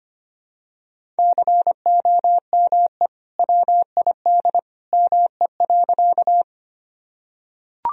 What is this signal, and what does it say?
Morse code, heard as a keyed tone, about 25 wpm, 700 Hz. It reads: COME WID ME.
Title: Sentences from Top 200 Words and Abbreviations 25wpm